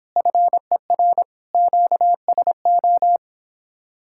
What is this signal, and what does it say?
Morse code, heard as a keyed tone, about 29 wpm, 700 Hz. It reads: FER QSO